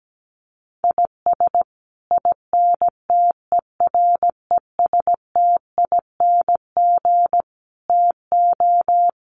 Morse code, heard as a keyed tone, about 17 wpm, 700 Hz. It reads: IS INTERESTING TO
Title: Sets of 5 Words Encore2 17wpm